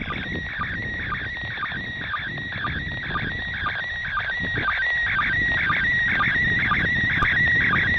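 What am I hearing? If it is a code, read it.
SSTV (slow-scan television), PD 120.
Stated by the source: SSTV image transmission from the ISS.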